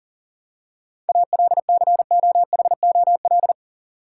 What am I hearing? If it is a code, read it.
Morse code, heard as a keyed tone, about 40 wpm, 700 Hz. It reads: ALCOHOL
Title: Seven Letter Words - 40wpm